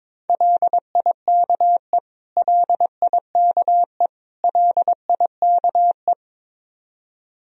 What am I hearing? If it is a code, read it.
Morse code, heard as a keyed tone, about 22 wpm, 700 Hz. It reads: LIKE LIKE LIKE
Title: Top 100 Words Repeated 22wpm